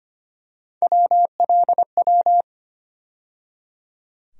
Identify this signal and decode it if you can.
Morse code, heard as a keyed tone, about 25 wpm, 700 Hz. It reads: WLW
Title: USA Call Sign Postfixes - 3 Letters 25wpm